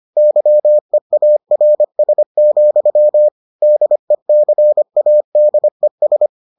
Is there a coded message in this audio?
Yes: Morse code from 0.2 to 6.3 s, heard as a keyed tone, about 25 wpm, 600 Hz. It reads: YEARS, DECADES